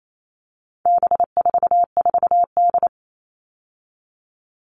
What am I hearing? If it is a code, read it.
Morse code, heard as a keyed tone, about 28 wpm, 700 Hz. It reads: B44D
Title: All Cut Numbers - 4-Digit 28wpm